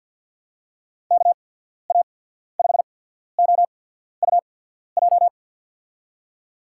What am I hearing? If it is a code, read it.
Morse code, heard as a keyed tone, about 53 wpm, spaced as for 14 wpm, 700 Hz. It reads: KA5OUJ
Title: USA Call Signs - 6 Characters - 8x Character Spacing - 50wpm